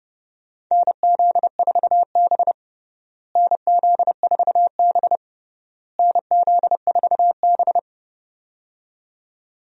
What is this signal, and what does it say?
Morse code, heard as a keyed tone, about 30 wpm, 700 Hz. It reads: NZ4B NZ4B NZ4B
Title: USA Call Signs - 4 Characters - Repeated 3x - 30wpm